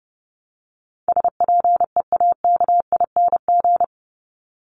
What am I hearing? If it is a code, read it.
Morse code, heard as a keyed tone, about 30 wpm, 700 Hz. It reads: SPEAKING
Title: ING Suffix - Top 100 Words with Base Words 30wpm